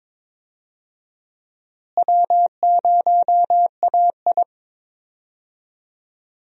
Morse code, heard as a keyed tone, about 22 wpm, 700 Hz. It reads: W0AI
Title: USA Call Signs - 4 Characters - Repeated 2x - 22wpm